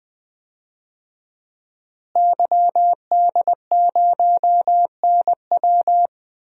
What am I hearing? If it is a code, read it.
Morse code, heard as a keyed tone, about 20 wpm, 700 Hz. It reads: YD0NW
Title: DXCC-100 Call Signs - 5 Characters - Repeated 3x - 20wpm